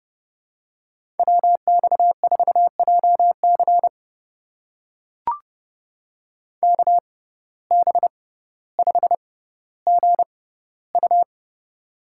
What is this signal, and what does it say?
Morse code, heard as a keyed tone, about 30 wpm, spaced as for 15 wpm, 700 Hz. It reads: WX4JC KB5GU
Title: USA Call Signs - 6x Character Spacing - 30wpm